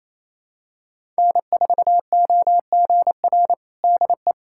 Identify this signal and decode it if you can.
Morse code, heard as a keyed tone, about 28 wpm, 700 Hz. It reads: N4OGR DE
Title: USA Callsign DE Callsign - 5 Characters - 28wpm